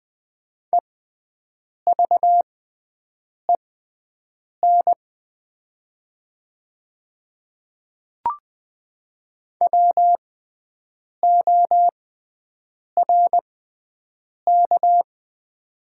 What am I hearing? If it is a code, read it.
Morse code, heard as a keyed tone, about 20 wpm, spaced as for 7 wpm, 700 Hz. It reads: EVEN WORK
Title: 6x Character Spacing - Top 300 Words - 20wpm